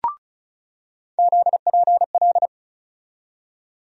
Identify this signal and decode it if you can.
Morse code, heard as a keyed tone, about 36 wpm, 700 Hz. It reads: ZPL
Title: USA Call Sign Postfixes - 3 Letters 35wpm